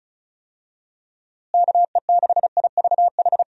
Morse code, heard as a keyed tone, about 36 wpm, 700 Hz. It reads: KE6IVH
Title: USA Call Signs - 6 Characters - Repeated 2x - 35wpm